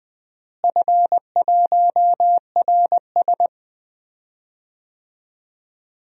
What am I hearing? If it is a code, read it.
Morse code, heard as a keyed tone, about 20 wpm, 700 Hz. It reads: F1RS
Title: DXCC-100 Call Signs - 4 Characters - Repeated 2x - 20wpm